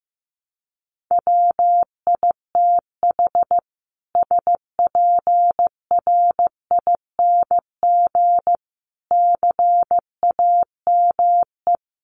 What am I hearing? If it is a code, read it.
Morse code, heard as a keyed tone, about 14 wpm, 700 Hz. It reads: WITH SPRING CAME
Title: Sentences from Top 1200 Words 15wpm